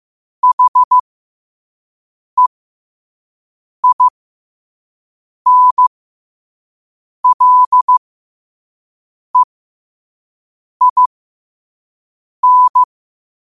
Morse code, heard as a keyed tone, about 15 wpm, spaced as for 5 wpm, 1000 Hz. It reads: HEINLEIN
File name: qotdfiles / 5wpm